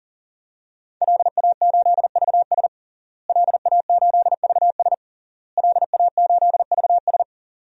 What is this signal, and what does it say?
Morse code, heard as a keyed tone, about 40 wpm, 700 Hz. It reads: LA8VS LA8VS LA8VS
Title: DXCC-100 Call Signs - 5 Characters - Repeated 3x - 40wpm